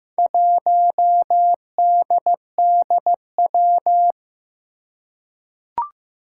Morse code, heard as a keyed tone, about 15 wpm, 700 Hz. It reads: 1DDW
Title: Four Challenging Letter-Number Combinations 15wpm